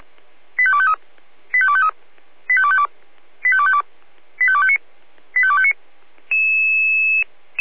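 Five-tone selective call (ZVEI), heard as quick runs of five tones, tones of about 70 ms. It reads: 85253 85253 85252 85252 85259 85259
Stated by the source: ZVEI five-tone selective call received by SDR.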